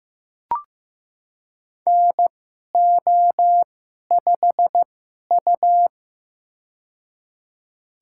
Morse code, heard as a keyed tone, about 15 wpm, spaced as for 11 wpm, 700 Hz. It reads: NO5U
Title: USA Call Signs - 4 Characters - 2x Character Spacing - 15wpm